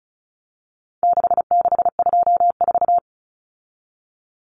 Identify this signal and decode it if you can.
Morse code, heard as a keyed tone, about 34 wpm, 700 Hz. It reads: 6624
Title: Numbers - 4-Digit 35wpm